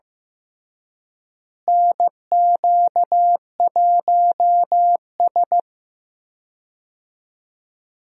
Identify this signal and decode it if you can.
Morse code, heard as a keyed tone, about 15 wpm, 700 Hz. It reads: NQ1S
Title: USA Call Signs - 4 Characters - Repeated 2x - 15wpm